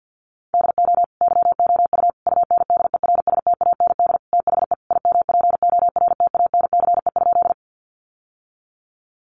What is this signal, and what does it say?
Morse code, heard as a keyed tone, about 50 wpm, 700 Hz. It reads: DO YOU UNDERSTAND THE IMPORTANCE?
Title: Sentences from Top 1200 Words 50wpm